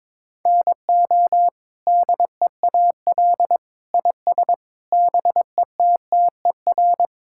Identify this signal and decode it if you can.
Morse code, heard as a keyed tone, about 22 wpm, 700 Hz. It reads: NO DEAL IS BETTER
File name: practice / 22wpm / Sets of 8 Words - 22wpm